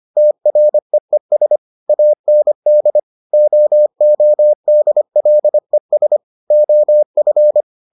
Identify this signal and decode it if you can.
Morse code, heard as a keyed tone, about 25 wpm, 600 Hz. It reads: TREES AND OODLES OF